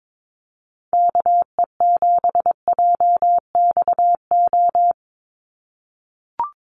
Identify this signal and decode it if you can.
Morse code, heard as a keyed tone, about 22 wpm, 700 Hz. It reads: KE7JXO
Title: USA Call Signs 22wpm